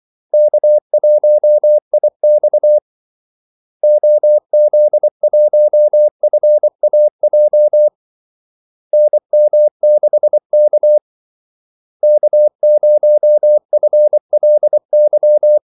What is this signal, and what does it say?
Morse code, heard as a keyed tone, about 24 wpm, 600 Hz. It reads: K1IX OZ1FAJ NM6K K0FLY
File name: Callsigns-24wpm